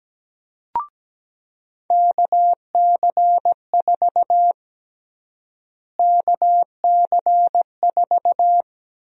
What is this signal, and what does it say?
Morse code, heard as a keyed tone, about 17 wpm, 700 Hz. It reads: KC4 KC4
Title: USA Call Sign Prefixes - Repeated 2x - 17wpm